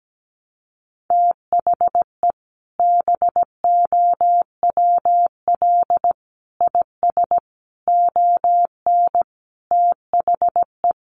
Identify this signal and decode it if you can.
Morse code, heard as a keyed tone, about 17 wpm, 700 Hz. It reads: THE BOWL IS ON THE
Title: Sentences from Top 1200 Words 17wpm